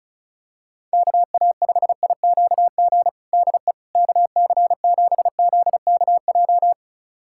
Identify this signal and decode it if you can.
Morse code, heard as a keyed tone, about 36 wpm, 700 Hz. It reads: KA5IQG DE KC7ZKJ
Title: USA Callsign DE Callsign - 6 Characters - 35wpm